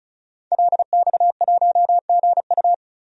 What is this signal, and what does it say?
Morse code, heard as a keyed tone, about 36 wpm, 700 Hz. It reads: LX1GU
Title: DXCC-100 Call Signs - 5 Characters - 35wpm